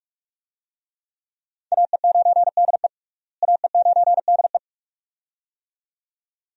Morse code, heard as a keyed tone, about 42 wpm, 700 Hz. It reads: AE9DE AE9DE
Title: USA Call Signs - 5 Characters - Repeated 2x - 45wpm